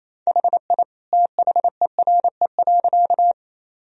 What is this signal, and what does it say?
Morse code, heard as a keyed tone, about 28 wpm, 700 Hz. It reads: HI THERE.
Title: Sentences from Top 1000 Words - Review Part 3 - 28wpm